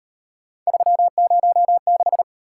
Morse code, heard as a keyed tone, about 33 wpm, 700 Hz. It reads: 306